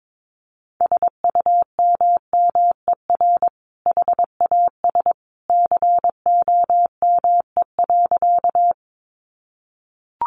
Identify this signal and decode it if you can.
Morse code, heard as a keyed tone, about 22 wpm, 700 Hz. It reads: SUMMER HAS COME.
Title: Sentences from Top 1100 Words 22wpm